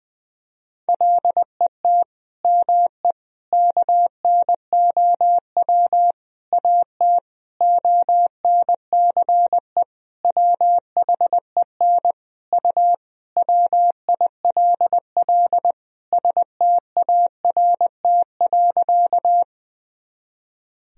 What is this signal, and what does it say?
Morse code, heard as a keyed tone, about 20 wpm, 700 Hz. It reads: LET ME KNOW AT ONCE WHEN U WILL START.
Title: Sentences from Top 300 Words and Abbreviations 20wpm